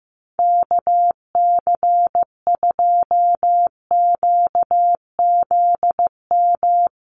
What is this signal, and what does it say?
Morse code, heard as a keyed tone, about 15 wpm, 700 Hz. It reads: KC2QZM
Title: USA Call Signs Repeated 2x 15wpm